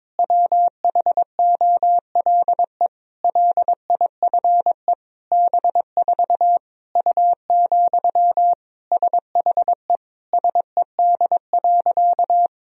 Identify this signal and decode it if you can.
Morse code, heard as a keyed tone, about 22 wpm, 700 Hz. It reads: WHOLE LIFE B4 U, SHE SED.
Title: Sentences from Top 400 Words and Abbreviations 22wpm